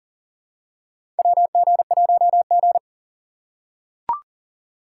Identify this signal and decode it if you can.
Morse code, heard as a keyed tone, about 40 wpm, 700 Hz. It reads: WG1G
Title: USA Call Signs - 2x Character Spacing - 40wpm